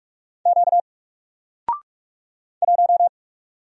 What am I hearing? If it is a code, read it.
Morse code, heard as a keyed tone, about 44 wpm, 700 Hz. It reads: Q 1
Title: Single Letter-Number Alt 45wpm